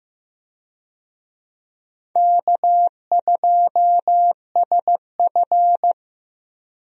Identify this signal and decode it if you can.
Morse code, heard as a keyed tone, about 15 wpm, 700 Hz. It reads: K2SF
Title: USA Call Signs - 4 Characters - Repeated 3x - 15wpm